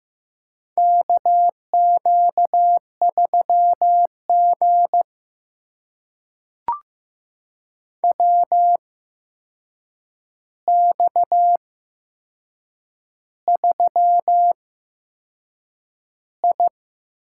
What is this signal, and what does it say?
Morse code, heard as a keyed tone, about 15 wpm, spaced as for 6 wpm, 700 Hz. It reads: KQ3G WX3I
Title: USA Call Signs - 4 Characters - 8x Character Spacing - 15wpm